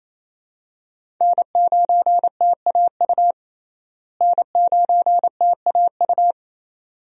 Morse code, heard as a keyed tone, about 27 wpm, 700 Hz. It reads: N9TAU N9TAU
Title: USA Call Signs - 5 Characters - Repeated 2x - 28wpm